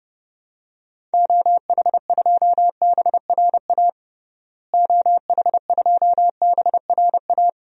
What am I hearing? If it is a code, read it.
Morse code, heard as a keyed tone, about 32 wpm, 700 Hz. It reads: OH2BRA OH2BRA
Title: DXCC-100 Call Signs - 6 Characters - Repeated 2x - 30wpm